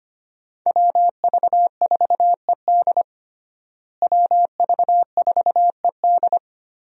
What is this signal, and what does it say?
Morse code, heard as a keyed tone, about 25 wpm, 700 Hz. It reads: WV4ED WV4ED
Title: USA Call Signs - 5 Characters - Repeated 2x - 25wpm